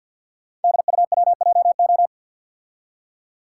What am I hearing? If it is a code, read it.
Morse code, heard as a keyed tone, about 50 wpm, 690 Hz. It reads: DUWJO
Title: Code Groups - 5 Letter:Numbers - 50wpm